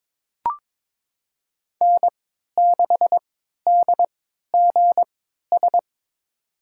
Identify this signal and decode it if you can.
Morse code, heard as a keyed tone, about 22 wpm, spaced as for 13 wpm, 700 Hz. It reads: N6DGS
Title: USA Call Signs - 5 Characters - 3x Character Spacing - 22wpm